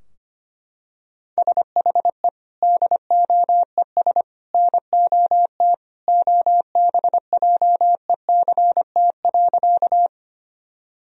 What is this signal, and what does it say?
Morse code, heard as a keyed tone, about 25 wpm, 700 Hz. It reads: SHE DOES NOT OBJECT.
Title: Sentences from Top 1000 Words - Review Part 3 - 25wpm